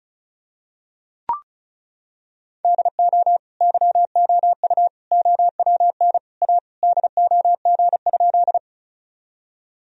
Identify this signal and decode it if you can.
Morse code, heard as a keyed tone, about 36 wpm, 700 Hz. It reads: DO YOU OWN A DOG?
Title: Sentences from Top 1000 Words - Review Part 2 - 35wpm